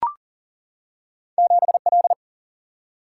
Morse code, heard as a keyed tone, about 40 wpm, 700 Hz. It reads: 7 L
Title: Single-Letter Single-Number 40wpm